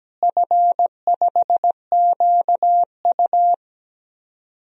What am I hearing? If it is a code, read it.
Morse code, heard as a keyed tone, about 17 wpm, 700 Hz. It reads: F5QU
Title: DXCC-100 Call Signs - 4 Characters - Repeated 3x - 17wpm